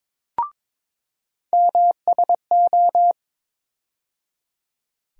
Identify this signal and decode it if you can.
Morse code, heard as a keyed tone, about 22 wpm, 700 Hz. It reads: MSO
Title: USA Call Sign Postfixes - 3 Letters 22wpm